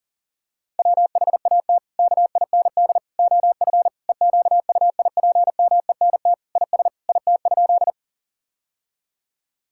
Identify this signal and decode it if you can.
Morse code, heard as a keyed tone, about 40 wpm, 700 Hz. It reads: WHAT KIND OF EQUIPMENT IS IT?
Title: Sentences from Top 1000 Words - Review Part 5 - 40wpm